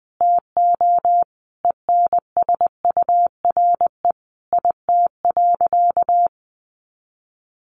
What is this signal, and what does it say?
Morse code, heard as a keyed tone, about 20 wpm, 700 Hz. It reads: TO ENSURE IT.
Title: Sentences from Top 1100 Words 20wpm